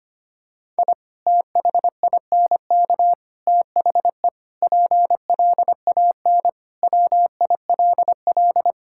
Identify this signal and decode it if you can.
Morse code, heard as a keyed tone, about 25 wpm, 700 Hz. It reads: I THINK THE PLAN WILL
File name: practice / 25wpm / Sentences with 3-Word Phrases - 25wpm